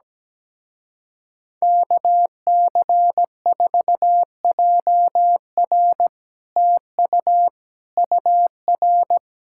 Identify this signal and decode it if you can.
Morse code, heard as a keyed tone, about 17 wpm, 700 Hz. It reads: KC4JR TU UR
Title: Speed-Racing Common QSO Phrases Encore - 17wpm